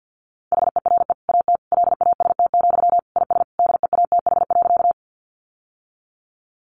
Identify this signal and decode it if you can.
Morse code, heard as a keyed tone, about 50 wpm, 700 Hz. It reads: HERE AT LAST, IS DEATH.
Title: Sentences from Top 1100 Words 50wpm